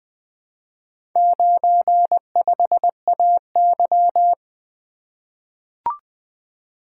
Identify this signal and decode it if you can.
Morse code, heard as a keyed tone, about 20 wpm, 700 Hz. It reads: 95AY
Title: Four Letter-Number Combinations - 20wpm